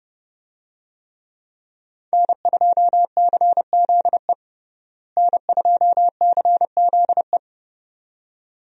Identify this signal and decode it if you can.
Morse code, heard as a keyed tone, about 30 wpm, 700 Hz. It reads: N2CZE N2CZE
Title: USA Call Signs - 5 Characters - Repeated 2x - 30wpm